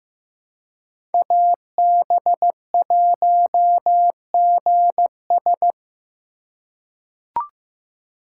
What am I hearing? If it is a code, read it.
Morse code, heard as a keyed tone, about 15 wpm, 700 Hz. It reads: AB1GS
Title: USA Call Signs - 2x Character Spacing - 15wpm